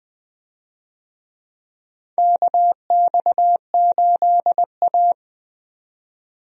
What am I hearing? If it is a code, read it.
Morse code, heard as a keyed tone, about 20 wpm, 700 Hz. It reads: KX8A
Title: USA Call Signs - 4 Characters - 20wpm